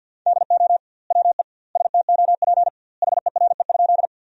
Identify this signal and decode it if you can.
Morse code, heard as a keyed tone, about 50 wpm, 700 Hz. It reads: DO WE STOP HERE?
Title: 2x Word-Spacing - Sentences from Top 300 Words - 50wpm